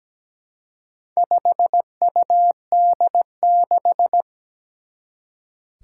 Morse code, heard as a keyed tone, about 17 wpm, 700 Hz. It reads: 5UD6
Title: Four Challenging Letter-Number Combinations 17wpm